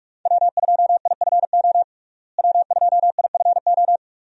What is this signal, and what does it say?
Morse code, heard as a keyed tone, about 46 wpm, 700 Hz. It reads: W2IFO W2IFO
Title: USA Call Signs - 5 Characters - Repeated 2x - 45wpm